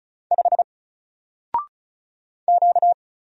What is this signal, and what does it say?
Morse code, heard as a keyed tone, about 36 wpm, 700 Hz. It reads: 5 Q
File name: Single-Letter-Number-35wpm